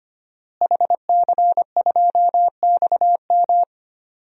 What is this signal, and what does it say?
Morse code, heard as a keyed tone, about 25 wpm, 700 Hz. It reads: HC2XM
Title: DXCC-100 Call Signs - 5 Characters - Repeated 2x - 25wpm